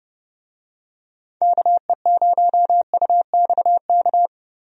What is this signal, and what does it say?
Morse code, heard as a keyed tone, about 30 wpm, 700 Hz. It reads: KE0UXK
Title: USA Call Signs - 6 Characters - 30wpm